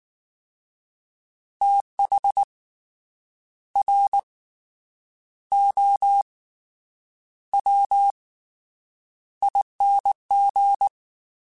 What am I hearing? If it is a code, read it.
Morse code, heard as a keyed tone, about 19 wpm, 800 Hz. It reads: TH R O W ING
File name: Fast TH @18